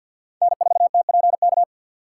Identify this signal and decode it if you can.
Morse code, heard as a keyed tone, about 52 wpm, 700 Hz. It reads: N4TPK
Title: USA Call Signs - 5 Characters - 2x Character Spacing - 50wpm